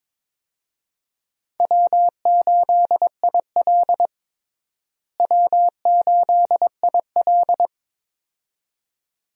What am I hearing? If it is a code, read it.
Morse code, heard as a keyed tone, about 22 wpm, 700 Hz. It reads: W8IL W8IL
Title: USA Call Signs - 4 Characters - Repeated 2x - 22wpm